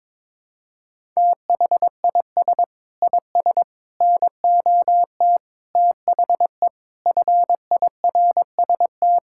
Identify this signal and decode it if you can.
Morse code, heard as a keyed tone, about 22 wpm, 700 Hz. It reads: THIS IS NOT THE FIRST
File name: practice / 22wpm / Sets of 6 Words - 22wpm